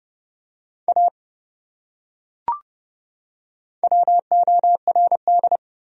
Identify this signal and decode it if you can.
Morse code, heard as a keyed tone, about 30 wpm, 700 Hz. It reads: A WORD
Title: Top 100 Words 30wpm